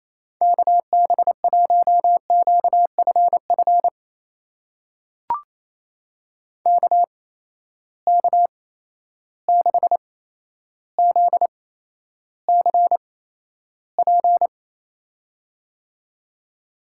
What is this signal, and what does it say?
Morse code, heard as a keyed tone, about 28 wpm, spaced as for 12 wpm, 700 Hz. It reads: KB1QFF KK6ZCP
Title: USA Call Signs - 6 Characters - 8x Character Spacing - 28wpm